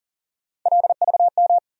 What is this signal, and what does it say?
Morse code, heard as a keyed tone, about 40 wpm, 700 Hz. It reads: LVM
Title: USA Call Sign Postfixes - 3 Letters 40wpm